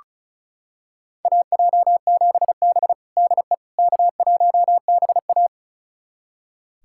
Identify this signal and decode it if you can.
Morse code, heard as a keyed tone, about 36 wpm, 700 Hz. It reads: AJ7B DE K1BA
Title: USA Callsign DE Callsign - 4 Characters - 35wpm